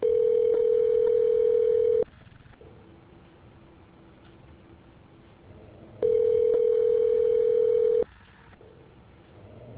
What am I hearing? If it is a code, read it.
Ringback tone (North America), 440 + 480 Hz.